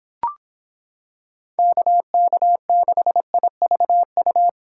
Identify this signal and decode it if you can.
Morse code, heard as a keyed tone, about 26 wpm, 700 Hz. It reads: KK6IVU